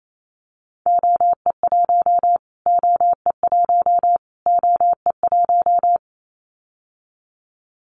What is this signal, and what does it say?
Morse code, heard as a keyed tone, about 28 wpm, 700 Hz. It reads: OE1 OE1 OE1
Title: Callsign Prefixes for Top 25 DXCC - Repeated 3x - 28wpm